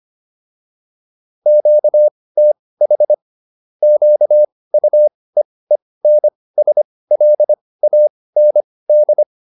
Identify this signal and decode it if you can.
Morse code, heard as a keyed tone, about 25 wpm, spaced as for 18 wpm, 600 Hz. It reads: QTH QUEENSLAND
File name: QSO407 18